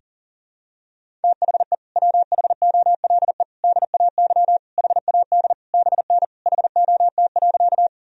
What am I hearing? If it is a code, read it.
Morse code, heard as a keyed tone, about 40 wpm, 700 Hz. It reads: THE WHOLE DAY HAD BN HOT.
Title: Sentences from Top 500 Words and Abbreviations 40wpm